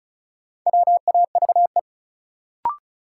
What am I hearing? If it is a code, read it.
Morse code, heard as a keyed tone, about 36 wpm, 700 Hz. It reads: WAVE
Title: Sentences from Top 1400 Words - New Word Drill - 35wpm